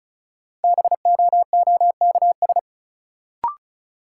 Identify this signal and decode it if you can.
Morse code, heard as a keyed tone, about 36 wpm, 700 Hz. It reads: BOOKS
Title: Sentences from Top 1000 Words - Review Part 4 - New Word Drill - 35wpm